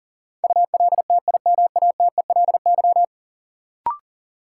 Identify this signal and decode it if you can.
Morse code, heard as a keyed tone, about 40 wpm, 700 Hz. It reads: ULTIMATELY
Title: Sentences from Top 1200 Words - New Word Drill Alt - 40wpm